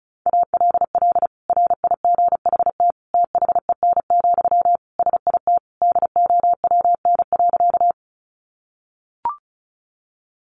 Morse code, heard as a keyed tone, about 34 wpm, 700 Hz. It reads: ALL RIGHT THEN, SIT DOWN.
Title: Sentences from Top 500 Words 35wpm